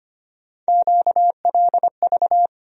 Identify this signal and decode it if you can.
Morse code, heard as a keyed tone, about 25 wpm, 700 Hz. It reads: QLV
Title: USA Call Sign Postfixes - 3 Letters 25wpm